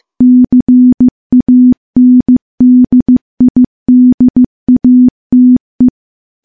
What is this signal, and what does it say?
Morse code, heard as a keyed tone, about 15 wpm, 250 Hz. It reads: CANDIDATE
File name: morse / morse-training-15WPM